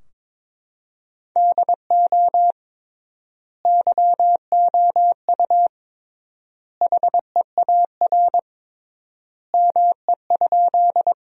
Morse code, heard as a keyed tone, about 22 wpm, 700 Hz. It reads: DO YOU HEAR ME?
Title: 3x Word-Spacing - Sentences from Top 300 Words - 22wpm